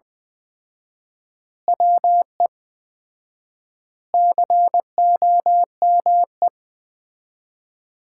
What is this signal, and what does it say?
Morse code, heard as a keyed tone, about 21 wpm, 700 Hz. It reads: WE COME
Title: 4x Word-Spacing - Sets of 5 Words - 20wpm